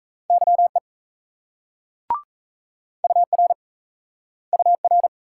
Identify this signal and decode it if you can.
Morse code, heard as a keyed tone, about 40 wpm, 700 Hz. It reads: YE UR UR